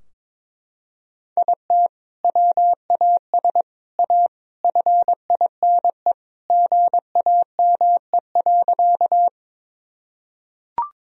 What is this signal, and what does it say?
Morse code, heard as a keyed tone, about 22 wpm, 700 Hz. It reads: IT WAS A FINE GAME.
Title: Sentences from Top 1000 Words - Review Part 3 - 22wpm